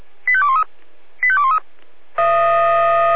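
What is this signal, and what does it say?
Five-tone selective call (ZVEI), heard as quick runs of five tones, tones of about 70 ms. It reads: 85213 85213
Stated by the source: ZVEI five-tone call 85211 received by SDR.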